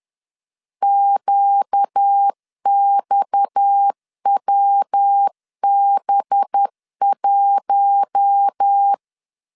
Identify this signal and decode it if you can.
Morse code, heard as a keyed tone, about 10 wpm, 800 Hz. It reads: QXWB1
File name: LEZIONI / 6 PKX0 / 6 PKX 10wpm 2426